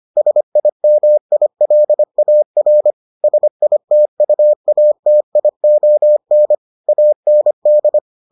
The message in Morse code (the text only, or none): SIMILAR SITUATION AND